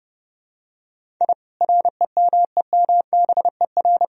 Morse code, heard as a keyed tone, about 30 wpm, 700 Hz. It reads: I REMEMBER
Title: Sentences from Top 1000 Words - Review Part 1 - 30wpm